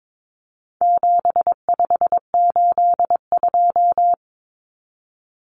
Morse code, heard as a keyed tone, about 22 wpm, 700 Hz. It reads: 7582
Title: Numbers - 4-Digit 22wpm